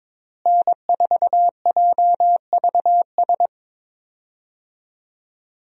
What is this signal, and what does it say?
Morse code, heard as a keyed tone, about 22 wpm, 700 Hz. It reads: N4JVS
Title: USA Call Signs - 5 Characters - Repeated 3x - 22wpm